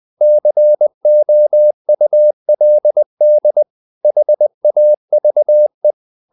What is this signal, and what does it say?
Morse code, heard as a keyed tone, about 20 wpm, 600 Hz. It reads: COULD HAVE
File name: the old man and the sea part 2 20wpm